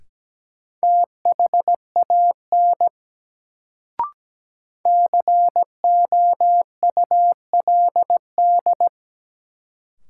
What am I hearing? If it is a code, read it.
Morse code, heard as a keyed tone, about 17 wpm, 700 Hz. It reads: THAN COULD
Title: Sentences from Top 100 Words - New Word Drill - 17wpm